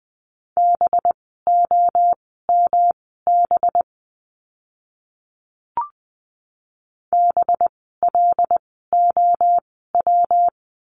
Morse code, heard as a keyed tone, about 20 wpm, spaced as for 14 wpm, 700 Hz. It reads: BOMB BLOW
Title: 185 Lesson 30f - Letter b - Words 20wpm